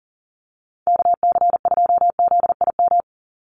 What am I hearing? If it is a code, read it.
Morse code, heard as a keyed tone, about 40 wpm, 700 Hz. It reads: KC2ZIM
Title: USA Call Signs 40wpm